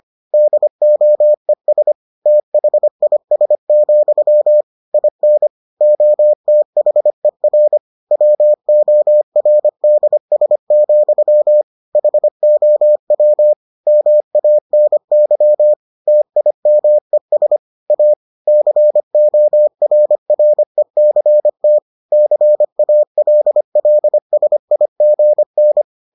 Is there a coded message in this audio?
Yes: Morse code from 0.3 to 25.8 s, heard as a keyed tone, about 25 wpm, 600 Hz. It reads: DOES THIS, IN OTHER WORDS, HOW MANY TIMES A CORRECT CALLSIGN